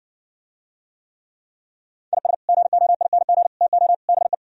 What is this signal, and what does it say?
Morse code, heard as a keyed tone, about 65 wpm, 700 Hz. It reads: IS GOING TO BE